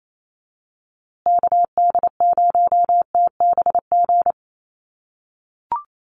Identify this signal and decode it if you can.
Morse code, heard as a keyed tone, about 28 wpm, 700 Hz. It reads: KD0TBG
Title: USA Call Signs Repeated 3x 28wpm